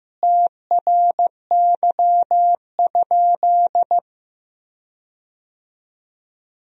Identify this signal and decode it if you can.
Morse code, heard as a keyed tone, about 15 wpm, 700 Hz. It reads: TRY?
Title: 4x Word-Spacing - Sentences from Top 200 Words - 15wpm